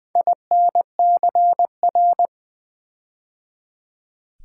Morse code, heard as a keyed tone, about 20 wpm, 700 Hz. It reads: INCR
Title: Build A Word - 4 Letters At A Time - 10-Letter Words - 20wpm